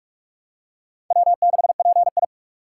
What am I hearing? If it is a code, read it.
Morse code, heard as a keyed tone, about 46 wpm, 700 Hz. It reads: W6PI
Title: USA Call Signs - 4 Characters - 2x Character Spacing - 45wpm